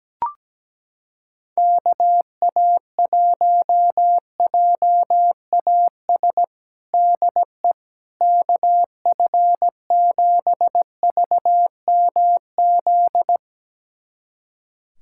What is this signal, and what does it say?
Morse code, heard as a keyed tone, about 17 wpm, 700 Hz. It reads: KA1JAS DE KF7VMZ
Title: USA Callsign DE Callsign - 6 Characters - 17wpm